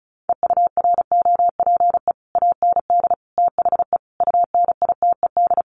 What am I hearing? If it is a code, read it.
Morse code, heard as a keyed tone, about 34 wpm, 700 Hz. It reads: EUROPE AND THE UNITED
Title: Sets of 5 Words Encore2 35wpm